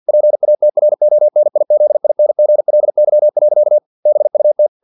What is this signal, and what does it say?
Morse code, heard as a keyed tone, about 50 wpm, 600 Hz. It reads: PATRONIZINGLY. BUT
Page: Never 49wpm